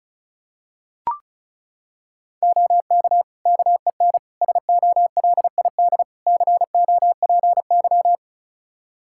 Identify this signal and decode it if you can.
Morse code, heard as a keyed tone, about 34 wpm, 700 Hz. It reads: OK KEN SOLID COPY
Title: Practice QSO - Exchange 3 - More Info - 35wpm